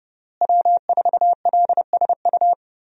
Morse code, heard as a keyed tone, about 30 wpm, 700 Hz. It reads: W4LSU
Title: USA Call Signs - 5 Characters - Repeated 3x - 30wpm